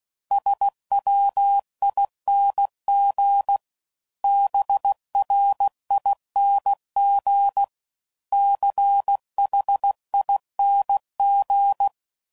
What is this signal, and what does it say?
Morse code, heard as a keyed tone, about 16 wpm, 800 Hz. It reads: SWING BRING CHING
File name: Suffixes / Special INGs / Sngl Ings